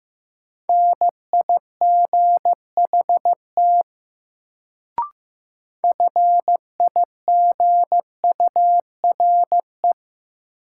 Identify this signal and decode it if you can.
Morse code, heard as a keyed tone, about 15 wpm, 700 Hz. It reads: NIGHT FIGURE
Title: Sets of 4 Words - New Word Drill - 15wpm